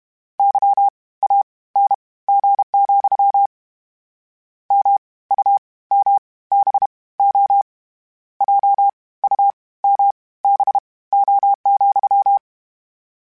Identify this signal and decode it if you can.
Morse code, heard as a keyed tone, about 32 wpm, spaced as for 19 wpm, 800 Hz. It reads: YANG, MUMBO JUMBO,
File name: Misc / Misc-mp3s / WordPairs sp @30